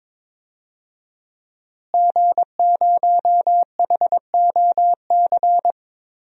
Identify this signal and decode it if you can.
Morse code, heard as a keyed tone, about 23 wpm, 700 Hz. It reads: G0HOC
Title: DXCC-100 Call Signs - 5 Characters - Repeated 3x - 22wpm